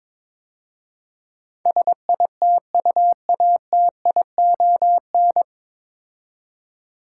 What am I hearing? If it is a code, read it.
Morse code, heard as a keyed tone, about 23 wpm, 700 Hz. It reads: SITUATION
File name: 9 Letter Words - First 3 Letters Given - 22wpm